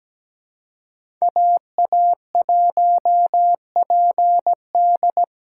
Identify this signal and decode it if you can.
Morse code, heard as a keyed tone, about 20 wpm, 700 Hz. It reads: AA1PD
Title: Speed Racing - USA 2x2 Call Signs 17wpm